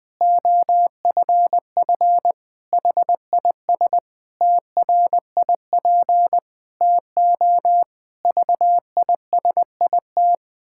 Morse code, heard as a keyed tone, about 20 wpm, 700 Hz. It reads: OFF HIS TRIP TO VISIT
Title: Pangram Challenge v2 - 20wpm